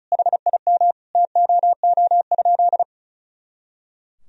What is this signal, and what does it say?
Morse code, heard as a keyed tone, about 36 wpm, 700 Hz. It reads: HIM TOO?
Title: Sentences from Top 500 Words and Abbreviations 35wpm